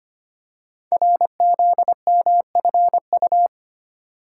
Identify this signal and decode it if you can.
Morse code, heard as a keyed tone, about 25 wpm, 700 Hz. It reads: RZMFU
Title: Code Groups - 5 Letter:Numbers - 25wpm